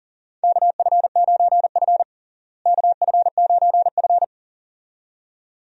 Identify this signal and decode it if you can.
Morse code, heard as a keyed tone, about 40 wpm, 700 Hz. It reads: KF9F KF9F
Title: USA Call Signs - 4 Characters - Repeated 2x - 40wpm